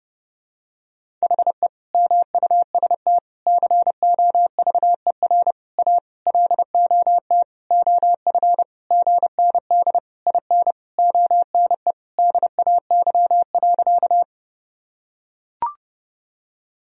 Morse code, heard as a keyed tone, about 30 wpm, 700 Hz. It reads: HE MUST COVER A LOT OF GND IN ONE DAY.
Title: Sentences from Top 500 Words and Abbreviations 30wpm